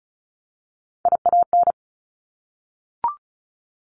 Morse code, heard as a keyed tone, about 36 wpm, 700 Hz. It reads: IAN
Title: Top Three Letter Combinations in Words 35wpm